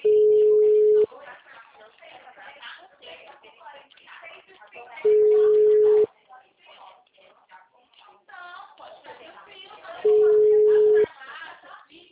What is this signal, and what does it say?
Ringback tone (Europe, 425 Hz), with speech.
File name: out- -2032